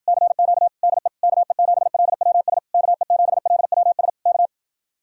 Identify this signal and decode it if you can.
Morse code, heard as a keyed tone, about 50 wpm, 700 Hz. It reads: CQ DE KE7LWS KE7LWS K
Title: Speed-Racing Common QSO Phrases - 40wpm